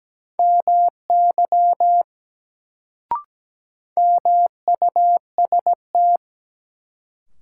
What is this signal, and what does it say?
Morse code, heard as a keyed tone, about 17 wpm, 700 Hz. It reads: MY MUST
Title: Sets of 2 Words - New Word Drill - 17wpm